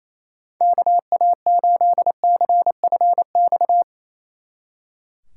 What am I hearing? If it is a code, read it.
Morse code, heard as a keyed tone, about 34 wpm, 700 Hz. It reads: KA8CFX
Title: Speed Racing - USA Call Signs 28wpm